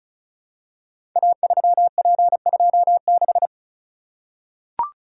Morse code, heard as a keyed tone, about 36 wpm, 700 Hz. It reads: A3P26
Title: Code Groups - 5 Letter:Numbers - 35wpm